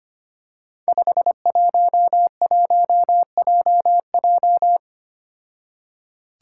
Morse code, heard as a keyed tone, about 25 wpm, 700 Hz. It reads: 511JJ
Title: Five Challenging Letter-Number Combinations 25wpm